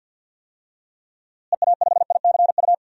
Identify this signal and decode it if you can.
Morse code, heard as a keyed tone, about 47 wpm, 700 Hz. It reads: EA5ICU
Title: DXCC-100 Call Signs - 6 Characters - Repeated 3x - 50wpm